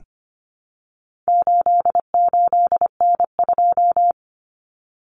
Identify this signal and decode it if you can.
Morse code, heard as a keyed tone, about 25 wpm, 700 Hz. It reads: 88N2
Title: Common Cut Numbers - 4-Digit 25wpm